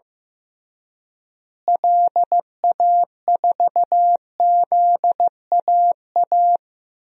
Morse code, heard as a keyed tone, about 15 wpm, 700 Hz. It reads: LA4ZAA
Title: DXCC-100 Call Signs - 6 Characters - Repeated 2x - 15wpm